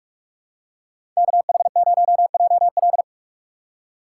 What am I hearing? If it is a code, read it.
Morse code, heard as a keyed tone, about 46 wpm, 690 Hz. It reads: KH0JL
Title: USA Call Signs - 5 Characters - 4x Character Spacing - 45wpm